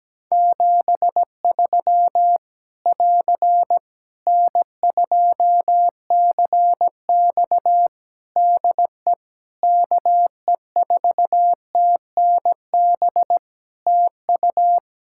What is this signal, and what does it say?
Morse code, heard as a keyed tone, about 17 wpm, 700 Hz. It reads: 73 <AR> N2CX DE KE4TNB TU
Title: Practice QSO - Exchange 4 - Ending - 17wpm